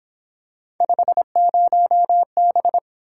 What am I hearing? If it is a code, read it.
Morse code, heard as a keyed tone, about 26 wpm, 700 Hz. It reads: 50B
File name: Speed-Racing - Three Letter-Number Combinations - 22wpm